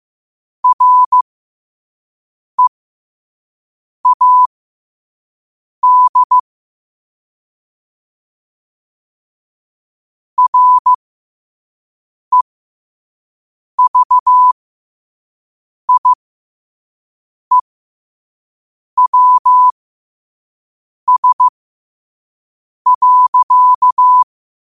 Morse code, heard as a keyed tone, about 15 wpm, spaced as for 5 wpm, 1000 Hz. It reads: READ REVIEWS.